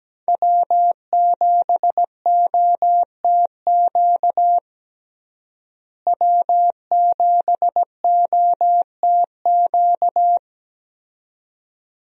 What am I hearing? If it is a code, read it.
Morse code, heard as a keyed tone, about 17 wpm, 700 Hz. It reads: W7OTQ W7OTQ
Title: USA Call Signs - 5 Characters - Repeated 2x - 17wpm